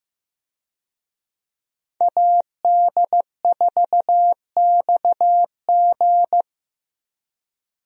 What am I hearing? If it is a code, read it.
Morse code, heard as a keyed tone, about 16 wpm, 700 Hz. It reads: AD4XG
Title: USA Call Signs - 5 Characters - Repeated 2x - 15wpm